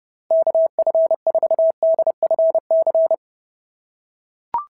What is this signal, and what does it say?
Morse code, heard as a keyed tone, about 26 wpm, 650 Hz. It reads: KF4DFC
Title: SOTA QSO Practice 30wpm